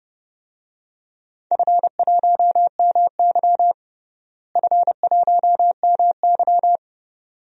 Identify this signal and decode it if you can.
Morse code, heard as a keyed tone, about 30 wpm, 700 Hz. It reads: F1MY F1MY
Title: DXCC-100 Call Signs - 4 Characters - Repeated 2x - 30wpm